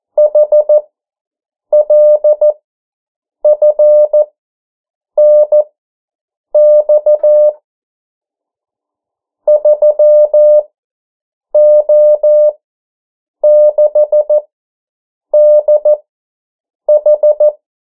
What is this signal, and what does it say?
Morse code, heard as a keyed tone, about 14 wpm, spaced as for 7 wpm, 600 Hz. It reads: HLFNX 3O6DH
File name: morse mp3 / 09 urok 8